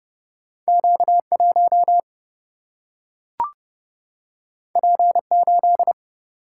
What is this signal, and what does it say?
Morse code, heard as a keyed tone, about 30 wpm, 700 Hz. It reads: Q1 P8
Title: Single-Letter Single-Number 30wpm